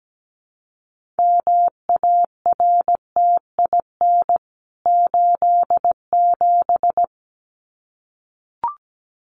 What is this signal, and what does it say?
Morse code, heard as a keyed tone, about 17 wpm, 700 Hz. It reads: MARTIN 87
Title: CWT Exchange 17wpm